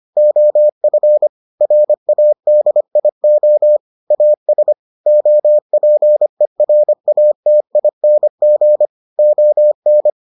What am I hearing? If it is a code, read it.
Morse code, heard as a keyed tone, about 25 wpm, 600 Hz. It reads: OF RADIO AS OPERATING ON